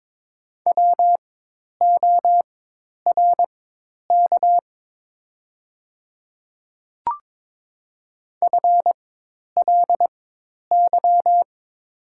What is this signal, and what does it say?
Morse code, heard as a keyed tone, about 22 wpm, spaced as for 10 wpm, 700 Hz. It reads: WORK FLY
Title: 4x Character Spacing - Top 500 Words - 22wpm